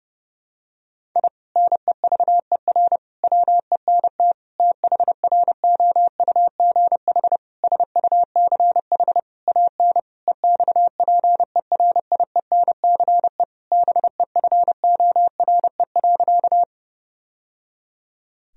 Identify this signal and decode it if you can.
Morse code, heard as a keyed tone, about 32 wpm, 700 Hz. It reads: I NEVER WENT THROUGH SUCH AN EXPERIENCE BEFORE.
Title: Sentences from Top 1000 Words - Review Part 3 - 30wpm